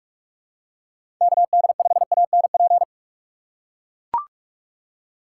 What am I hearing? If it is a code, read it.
Morse code, heard as a keyed tone, about 44 wpm, 700 Hz. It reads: KD5ANP
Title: USA Call Signs - 3x Character Spacing - 45wpm